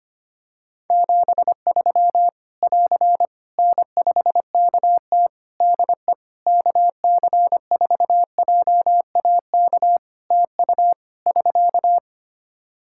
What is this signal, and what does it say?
Morse code, heard as a keyed tone, about 25 wpm, 700 Hz. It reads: 73 <AR> N5KT DE KC4JAK TU <SK>
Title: K1USN SST QSO Practice 25wpm